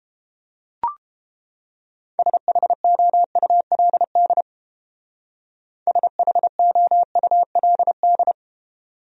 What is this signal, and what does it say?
Morse code, heard as a keyed tone, about 29 wpm, 700 Hz. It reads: SHOULD SHOULD